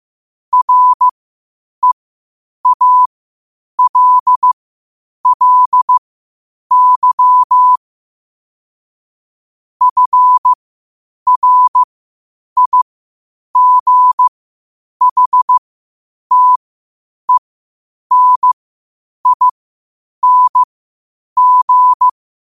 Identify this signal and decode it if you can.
Morse code, heard as a keyed tone, about 15 wpm, spaced as for 8 wpm, 1000 Hz. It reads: REALLY FRIGHTENING
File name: qotdfiles / 8wpm